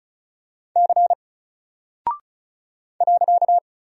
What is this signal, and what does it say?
Morse code, heard as a keyed tone, about 36 wpm, 700 Hz. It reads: C .
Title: Single Character (Letter:Number:Punctuation:Prosign) Alt 35wpm